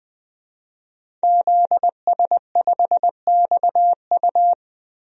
Morse code, heard as a keyed tone, about 20 wpm, 700 Hz. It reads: ZS5XU
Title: DXCC-100 Call Signs - 5 Characters - Repeated 3x - 20wpm